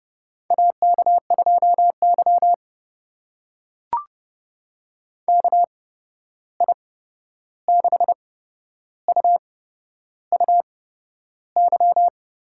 Morse code, heard as a keyed tone, about 32 wpm, spaced as for 11 wpm, 700 Hz. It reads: AK2Y KI6UUY
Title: USA Call Signs - 8x Character Spacing - 30wpm